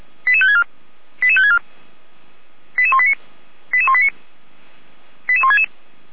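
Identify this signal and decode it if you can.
Five-tone selective call (ZVEI), heard as quick runs of five tones, tones of about 70 ms. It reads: 88564 88564 88189 88189 88166
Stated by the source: Five-tone ZVEI selective call received by SDR.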